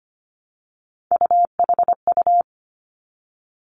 Morse code, heard as a keyed tone, about 25 wpm, 700 Hz. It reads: UHU
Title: Three Challenging Letter-Number Combinations 25wpm